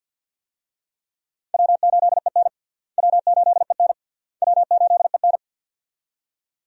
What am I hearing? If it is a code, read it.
Morse code, heard as a keyed tone, about 49 wpm, 700 Hz. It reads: W8EN W8EN W8EN
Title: USA Call Signs - 4 Characters - Repeated 3x - 50wpm